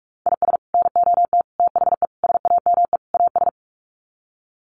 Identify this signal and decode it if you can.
Morse code, heard as a keyed tone, about 44 wpm, 700 Hz. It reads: IS NOT THE SAME AS
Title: Sets of 5 Words Encore 45wpm